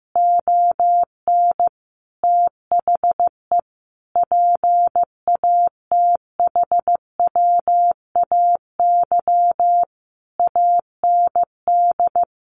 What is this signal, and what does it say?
Morse code, heard as a keyed tone, about 15 wpm, 700 Hz. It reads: ON THE PATHWAY AND
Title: A Sound Of Thunder 15wpm